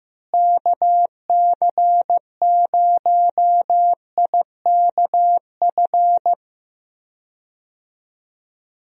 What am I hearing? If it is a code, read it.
Morse code, heard as a keyed tone, about 15 wpm, 700 Hz. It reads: KC0IKF
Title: USA Call Signs - 6 Characters - Repeated 3x - 15wpm